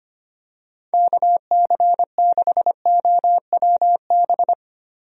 Morse code, heard as a keyed tone, about 25 wpm, 700 Hz. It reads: KC6OWB
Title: USA Call Signs - 6 Characters - Repeated 3x - 25wpm